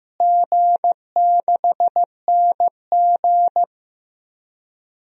Morse code, heard as a keyed tone, about 15 wpm, 700 Hz. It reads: G6NG
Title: DXCC-100 Call Signs - 4 Characters - Repeated 2x - 15wpm